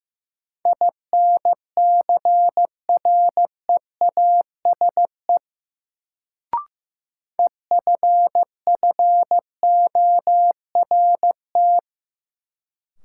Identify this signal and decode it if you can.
Morse code, heard as a keyed tone, about 15 wpm, 700 Hz. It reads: INCREASE EFFORT
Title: Sentences from Top 1200 Words - New Word Drill Alt - 15wpm